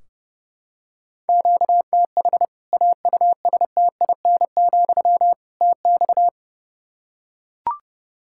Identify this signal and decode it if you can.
Morse code, heard as a keyed tone, about 30 wpm, 700 Hz. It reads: QTH AUSTIN, TX
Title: Practice QSO - Exchange 1 - Beginning - 30wpm